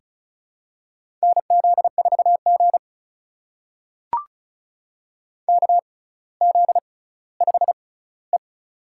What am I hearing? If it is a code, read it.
Morse code, heard as a keyed tone, about 36 wpm, spaced as for 18 wpm, 700 Hz. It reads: NZ4G KZ5E
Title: USA Call Signs - 4 Characters - 6x Character Spacing - 35wpm